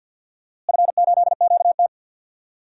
Morse code, heard as a keyed tone, about 51 wpm, 700 Hz. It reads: F8QT
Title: DXCC-100 Call Signs - 4 Characters - Repeated 3x - 50wpm